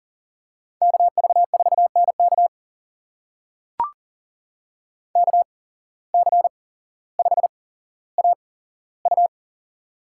Morse code, heard as a keyed tone, about 40 wpm, spaced as for 17 wpm, 700 Hz. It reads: KV4NK KC5AU
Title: USA Call Signs - 5 Characters - 8x Character Spacing - 40wpm